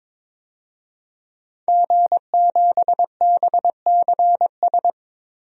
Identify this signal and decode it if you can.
Morse code, heard as a keyed tone, about 22 wpm, 700 Hz. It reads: G7BCS
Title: DXCC-100 Call Signs - 5 Characters - Repeated 3x - 22wpm